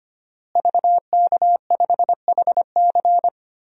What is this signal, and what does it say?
Morse code, heard as a keyed tone, about 25 wpm, 700 Hz. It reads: VK5HC
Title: DXCC-100 Call Signs - 5 Characters - Repeated 3x - 25wpm